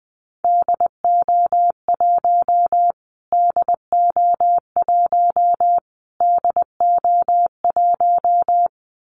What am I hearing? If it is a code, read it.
Morse code, heard as a keyed tone, about 20 wpm, 700 Hz. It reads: DO1 DO1 DO1
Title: Callsign Prefixes for Top 25 DXCC - Repeated 3x - 20wpm